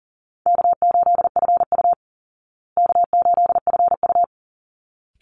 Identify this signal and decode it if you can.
Morse code, heard as a keyed tone, about 40 wpm, 700 Hz. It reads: K8FU K8FU
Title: Easy Call Signs Repeated 2x - 40wpm